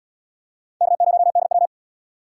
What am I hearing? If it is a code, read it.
Morse code, heard as a keyed tone, about 70 wpm, 700 Hz. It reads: K0DW